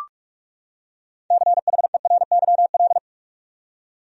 Morse code, heard as a keyed tone, about 45 wpm, 700 Hz. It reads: CHERYL
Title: USA Common Names - 45wpm